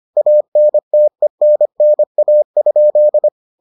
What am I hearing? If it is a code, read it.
Morse code, heard as a keyed tone, about 25 wpm, 600 Hz. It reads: ANTENNA?